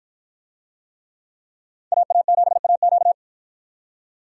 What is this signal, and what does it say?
Morse code, heard as a keyed tone, about 53 wpm, 700 Hz. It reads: AA7AQ